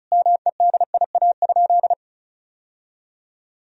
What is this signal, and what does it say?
Morse code, heard as a keyed tone, about 34 wpm, 700 Hz. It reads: MEDIA?
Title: Sentences from Top 1000 Words - Review Part 4 - 35wpm